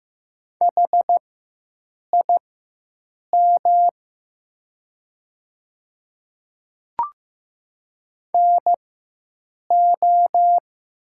Morse code, heard as a keyed tone, about 15 wpm, spaced as for 7 wpm, 700 Hz. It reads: HIM NO
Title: 4x Character Spacing - Top 500 Words - 15wpm